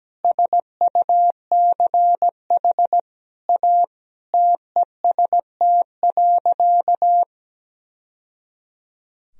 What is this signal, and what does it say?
Morse code, heard as a keyed tone, about 17 wpm, 700 Hz. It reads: SUCH A TEST.
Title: Sentences from Top 1000 Words - Review Part 3 - 17wpm